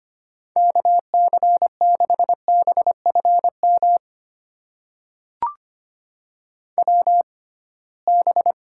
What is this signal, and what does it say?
Morse code, heard as a keyed tone, about 25 wpm, spaced as for 19 wpm, 700 Hz. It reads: KC6BFM WB
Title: USA Call Signs - 6 Characters - 6x Character Spacing - 25wpm